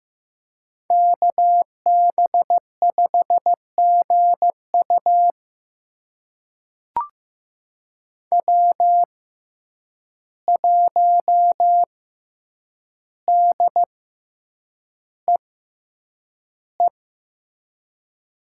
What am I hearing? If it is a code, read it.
Morse code, heard as a keyed tone, about 15 wpm, spaced as for 8 wpm, 700 Hz. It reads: KB5GU W1DEE
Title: USA Call Signs - 5 Characters - 6x Character Spacing - 15wpm